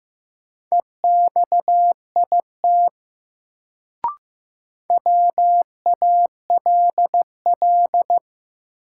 Morse code, heard as a keyed tone, about 15 wpm, 700 Hz. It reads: EXIT WALL
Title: Sentences from Top 1100 Words - New Word Drill Alt - 15wpm